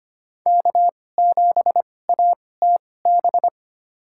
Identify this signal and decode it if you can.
Morse code, heard as a keyed tone, about 25 wpm, spaced as for 18 wpm, 700 Hz. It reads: K7ATB
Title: USA Call Signs - 2x Character Spacing - 25wpm